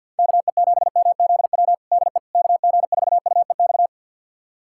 Morse code, heard as a keyed tone, about 52 wpm, 700 Hz. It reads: KE7MZW DE KG4UEX
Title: USA Callsign DE Callsign - 6 Characters - 50wpm